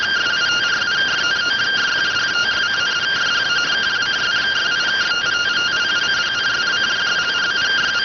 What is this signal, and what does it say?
Two-tone teleprinter signal, about 170 Hz shift.